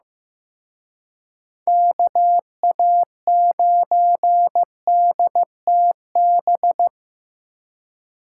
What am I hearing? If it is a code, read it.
Morse code, heard as a keyed tone, about 18 wpm, 700 Hz. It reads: KA9DTB
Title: Speed Racing - USA 2x3 Call Signs 15wpm